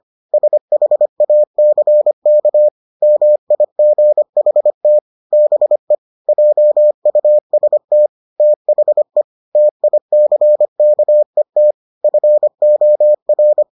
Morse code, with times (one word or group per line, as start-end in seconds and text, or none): 0.3-2.7 SHACK
3.0-5.0 MIGHT
5.3-6.0 BE
6.3-8.1 JUST
8.4-9.2 THE
9.6-11.7 TICKET
12.0-13.6 FOR